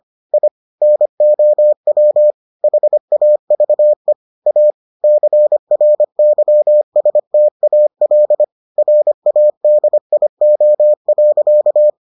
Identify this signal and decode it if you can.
Morse code, heard as a keyed tone, about 25 wpm, 600 Hz. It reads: I NOW HAVE A CRYSTAL RADIO.